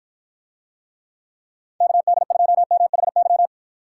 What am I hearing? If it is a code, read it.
Morse code, heard as a keyed tone, about 53 wpm, 700 Hz. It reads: KD2MHY